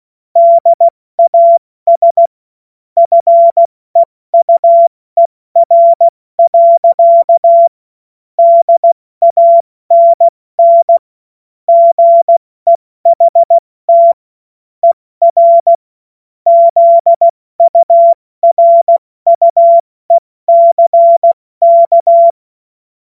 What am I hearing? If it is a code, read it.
Morse code, heard as a keyed tone, about 16 wpm, 680 Hz. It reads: DAS FEUER. DANN GEHT ER ZURUECK
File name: Morsekurs / Witze / Witz 14. Juni 2022 70 F80